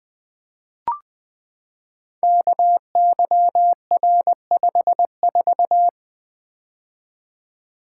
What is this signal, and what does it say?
Morse code, heard as a keyed tone, about 20 wpm, 700 Hz. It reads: KYR54
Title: Code Groups - 5 Letter:Numbers - 20wpm